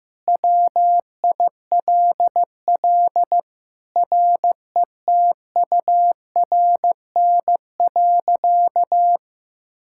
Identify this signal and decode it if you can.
Morse code, heard as a keyed tone, about 15 wpm, 700 Hz. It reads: WILL RETURN.
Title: Sentences from Top 1000 Words - Review Part 6 - 15wpm